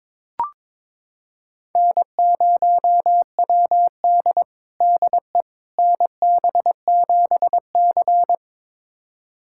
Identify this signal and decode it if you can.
Morse code, heard as a keyed tone, about 22 wpm, 700 Hz. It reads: N0WD DE NB7C
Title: USA Callsign DE Callsign - 4 Characters - 22wpm